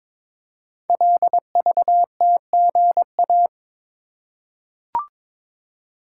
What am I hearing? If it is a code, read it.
Morse code, heard as a keyed tone, about 22 wpm, 700 Hz. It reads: LVTGA
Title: Code Groups - 5 Letters - 22wpm